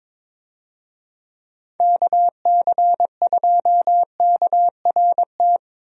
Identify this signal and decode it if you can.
Morse code, heard as a keyed tone, about 22 wpm, 700 Hz. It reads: KC2KRT
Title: USA Call Signs - 6 Characters - Repeated 2x - 22wpm